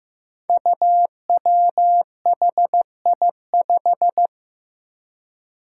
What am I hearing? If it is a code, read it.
Morse code, heard as a keyed tone, about 15 wpm, 700 Hz. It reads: UWHI5
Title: Five Challenging Letter-Number Combinations 15wpm